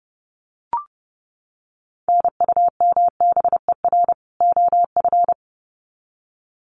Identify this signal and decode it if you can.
Morse code, heard as a keyed tone, about 30 wpm, 700 Hz. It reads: NUMBER OF
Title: Sets of 2 Words 30wpm